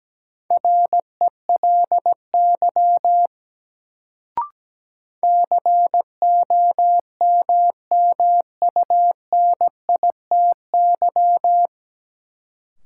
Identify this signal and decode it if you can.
Morse code, heard as a keyed tone, about 17 wpm, 700 Hz. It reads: RELY COMMUNITY
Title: Sentences from Top 1200 Words - New Word Drill Alt - 17wpm